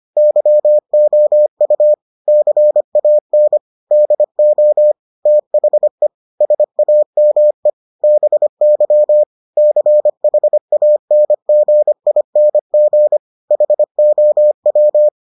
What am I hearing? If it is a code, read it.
Morse code, heard as a keyed tone, about 25 wpm, 600 Hz. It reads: YOU CAN DO THE SAME BY CHANGING HOW